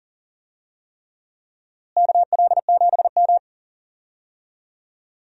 Morse code, heard as a keyed tone, about 40 wpm, 700 Hz. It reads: KL7M
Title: DXCC-100 Call Signs - 4 Characters - Repeated 3x - 40wpm